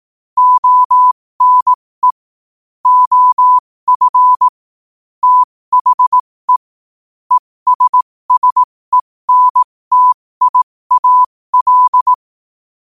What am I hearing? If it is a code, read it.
Morse code, heard as a keyed tone, about 18 wpm, spaced as for 15 wpm, 1000 Hz. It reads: ONE OF THE ESSENTIAL
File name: qotdfiles / 15wpm